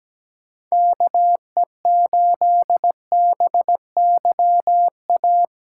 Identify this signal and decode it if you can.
Morse code, heard as a keyed tone, about 17 wpm, 700 Hz. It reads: KE8BY A
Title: USA Call Signs - 8x Character Spacing - 17wpm